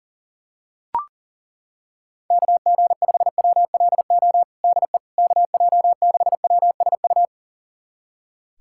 Morse code, heard as a keyed tone, about 38 wpm, 700 Hz. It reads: KG5WLO DE KJ6WSU
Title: USA Callsign DE Callsign 40wpm